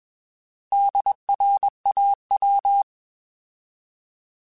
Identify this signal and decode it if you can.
Morse code, heard as a keyed tone, about 21 wpm, 800 Hz. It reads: DRAW
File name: Alphabetical / Alphabetical mp3s / Ds @20